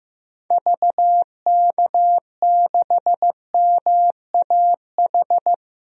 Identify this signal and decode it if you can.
Morse code, heard as a keyed tone, about 15 wpm, 700 Hz. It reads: VK6MAH
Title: DXCC-100 Call Signs - 6 Characters - 15wpm